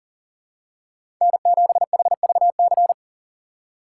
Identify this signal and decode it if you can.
Morse code, heard as a keyed tone, about 37 wpm, 700 Hz. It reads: N7HVC
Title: USA Call Signs - 4x Character Spacing - 40wpm